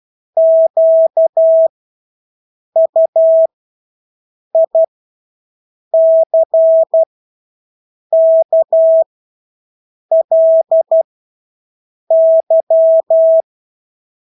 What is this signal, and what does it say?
Morse code, heard as a keyed tone, about 12 wpm, spaced as for 6 wpm, 650 Hz. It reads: QUICKLY
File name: Engelske ord 12-6 wpm